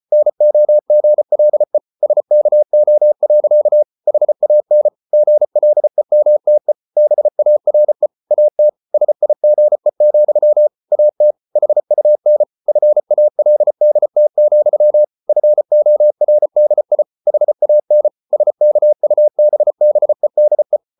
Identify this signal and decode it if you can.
Morse code, heard as a keyed tone, about 34 wpm, 600 Hz. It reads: NOGLE SKO. HAN GLEMTE BARE AT SIGE, AT HUN FALDT, FORDI HAN SKUBBEDE
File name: Ferie QRQ 34wpm